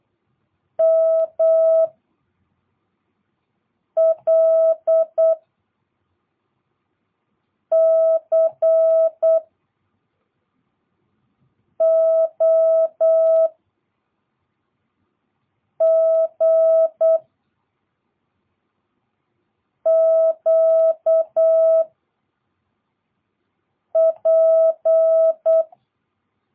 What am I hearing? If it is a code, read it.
Morse code, heard as a keyed tone, about 8 wpm, 660 Hz. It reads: M L C O G Q P